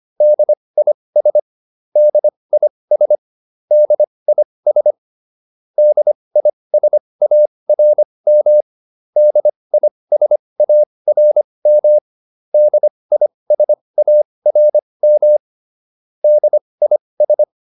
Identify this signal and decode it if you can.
Morse code, heard as a keyed tone, about 25 wpm, spaced as for 20 wpm, 600 Hz. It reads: DIS DIS DIS DISARM DISARM DISARM DIS